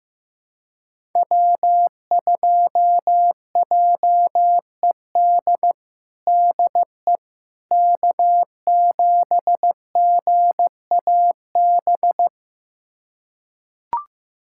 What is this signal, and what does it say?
Morse code, heard as a keyed tone, about 16 wpm, 700 Hz. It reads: W2JED DE K7GAB
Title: USA Callsign DE Callsign - 5 Characters - 15wpm